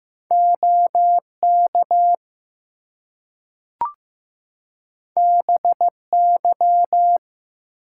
Morse code, heard as a keyed tone, about 15 wpm, 700 Hz. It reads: OK BY
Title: Two Letter Words - 15wpm